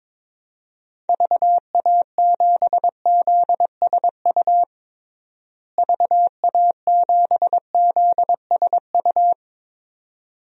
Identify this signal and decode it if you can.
Morse code, heard as a keyed tone, about 21 wpm, 700 Hz. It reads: VA7ZSU VA7ZSU
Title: DXCC-100 Call Signs - 6 Characters - Repeated 2x - 22wpm